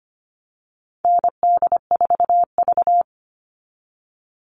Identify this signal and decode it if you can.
Morse code, heard as a keyed tone, about 25 wpm, 700 Hz. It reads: ND4V
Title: Call Signs Repeated 2x - 25wpm